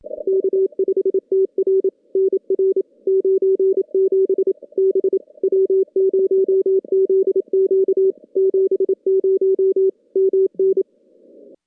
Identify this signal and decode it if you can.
Morse code, heard as a keyed tone, about 28 wpm, 410 Hz. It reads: K5TR NR 97 B W0ZQ 70 MN
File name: contests / CW-SS-k5tr- -2300